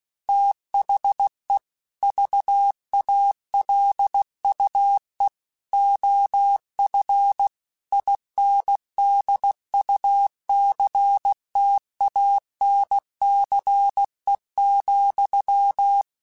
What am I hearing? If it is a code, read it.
Morse code, heard as a keyed tone, about 16 wpm, 800 Hz. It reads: THE VALUE OF INDUCTANCE,
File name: Technical Subjects / Technical Subjects-mp3s / XL @15